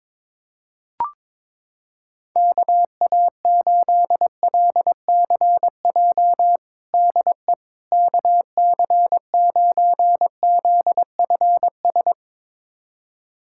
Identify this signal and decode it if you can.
Morse code, heard as a keyed tone, about 22 wpm, 700 Hz. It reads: KA8LCJ DE KC9ZFS
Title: USA Callsign DE Callsign - 6 Characters - 22wpm